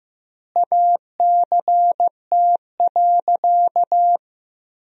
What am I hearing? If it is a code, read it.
Morse code, heard as a keyed tone, about 15 wpm, 700 Hz. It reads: ACT.
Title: 2x Word-Spacing - Sentences from Top 200 Words - 15wpm